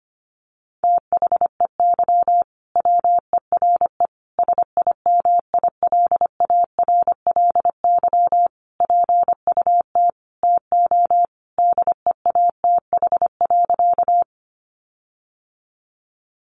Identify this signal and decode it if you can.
Morse code, heard as a keyed tone, about 25 wpm, 700 Hz. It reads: THEY WERE SIMILARLY PUT TO DEATH.
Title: Sentences from Top 1400 Words 25wpm